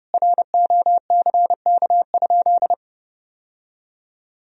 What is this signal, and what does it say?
Morse code, heard as a keyed tone, about 30 wpm, 700 Hz. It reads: ROCK?
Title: Sentences from Top 1000 Words - Review Part 2 - 30wpm